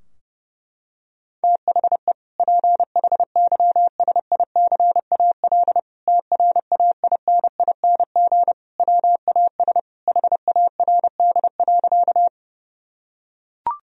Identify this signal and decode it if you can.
Morse code, heard as a keyed tone, about 30 wpm, 700 Hz. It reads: THE PHYSICAL TRAINING WAS HARD.
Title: Sentences from Top 1000 Words - Review Part 4 - 30wpm